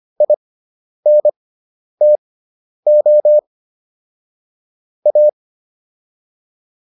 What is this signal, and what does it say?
Morse code, heard as a keyed tone, about 25 wpm, spaced as for 10 wpm, 600 Hz. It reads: INTO A